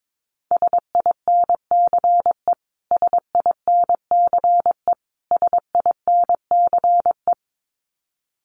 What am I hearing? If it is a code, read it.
Morse code, heard as a keyed tone, about 22 wpm, 700 Hz. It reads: SINCE SINCE SINCE
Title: Top 500 Words Repeated 22wpm